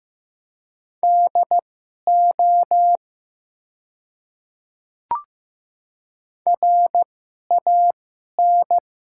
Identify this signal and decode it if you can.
Morse code, heard as a keyed tone, about 15 wpm, spaced as for 11 wpm, 700 Hz. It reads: DO RAN
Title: 2x Character Spacing - Top 500 Words - 15wpm